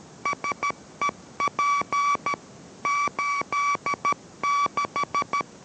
Morse code, heard as a keyed tone, about 13 wpm, 1140 Hz. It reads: SEP 86